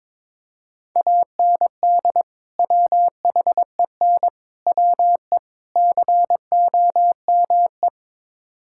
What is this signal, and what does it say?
Morse code, heard as a keyed tone, about 22 wpm, 700 Hz. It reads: AND WHEN WE COME
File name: practice / 22wpm / Build A Phrase - 5-Word Phrases - 22wpm